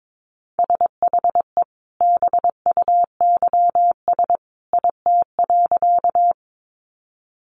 Morse code, heard as a keyed tone, about 22 wpm, 700 Hz. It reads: SHE BUYS IT.
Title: Additional Easy Phrases 22wpm